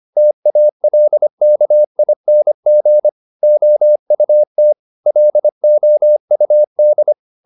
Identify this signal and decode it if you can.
Morse code, heard as a keyed tone, about 25 wpm, 600 Hz. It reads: TALKING OUT LOUD
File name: the old man and the sea part 1 25wpm